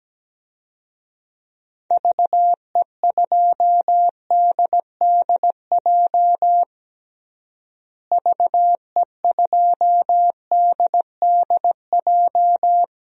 Morse code, heard as a keyed tone, about 17 wpm, 700 Hz. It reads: VE2DDJ VE2DDJ
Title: DXCC-100 Call Signs - 6 Characters - Repeated 3x - 17wpm